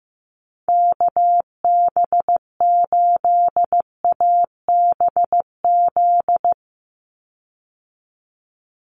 Morse code, heard as a keyed tone, about 15 wpm, 700 Hz. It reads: KB8ABZ
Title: USA Call Signs 15wpm